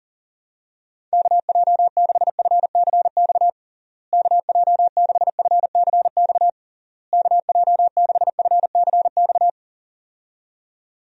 Morse code, heard as a keyed tone, about 40 wpm, 700 Hz. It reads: KJ6FCX KJ6FCX KJ6FCX
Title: USA Call Signs - 6 Characters - Repeated 3x - 40wpm